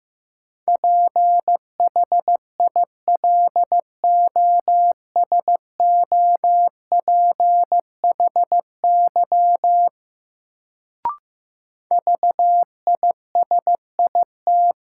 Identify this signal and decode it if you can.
Morse code, heard as a keyed tone, about 16 wpm, 700 Hz. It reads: PHILOSOPHY VISIT
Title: Sentences from Top 900 Words - New Word Drill - 15wpm